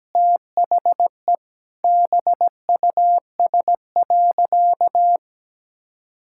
Morse code, heard as a keyed tone, about 17 wpm, 700 Hz. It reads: THE BUS.
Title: Sentences from Top 1000 Words - Review Part 6 - 17wpm